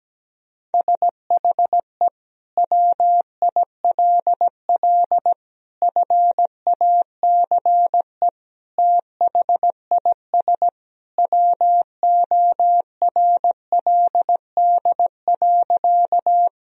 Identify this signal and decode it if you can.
Morse code, heard as a keyed tone, about 17 wpm, 700 Hz. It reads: SHE WILL FACE THIS WORLD.
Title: Sentences from Top 1000 Words - Review Part 2 - 17wpm